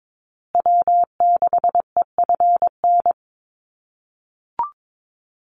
Morse code, heard as a keyed tone, about 22 wpm, 700 Hz. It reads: W6EFN
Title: USA Call Signs Repeated 3x 22wpm